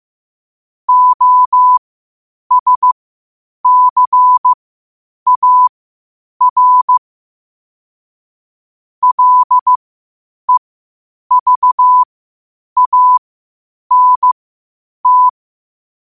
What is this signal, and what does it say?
Morse code, heard as a keyed tone, about 15 wpm, spaced as for 8 wpm, 1000 Hz. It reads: OSCAR LEVANT
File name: qotdfiles / 8wpm